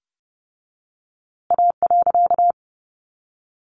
Morse code, heard as a keyed tone, about 30 wpm, 700 Hz. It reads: A.
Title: A Sound Of Thunder 30wpm